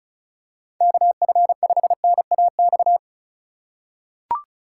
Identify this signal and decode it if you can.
Morse code, heard as a keyed tone, about 35 wpm, 700 Hz. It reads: KF5NAX
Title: USA Call Signs - 4x Character Spacing - 35wpm